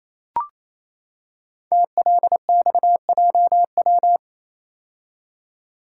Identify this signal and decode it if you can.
Morse code, heard as a keyed tone, about 27 wpm, 700 Hz. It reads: TLXJW
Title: Code Groups - 5 Letters - 28wpm